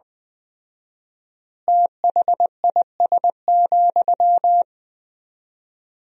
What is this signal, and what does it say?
Morse code, heard as a keyed tone, about 20 wpm, 700 Hz. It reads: THIS,
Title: 4x Word-Spacing - Sentences from Top 400 Words - 20wpm